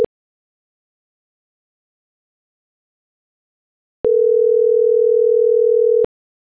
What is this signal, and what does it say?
Ringback tone (North America), 440 + 480 Hz.